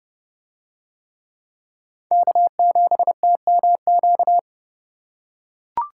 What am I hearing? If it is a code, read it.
Morse code, heard as a keyed tone, about 29 wpm, 700 Hz. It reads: K7TMQ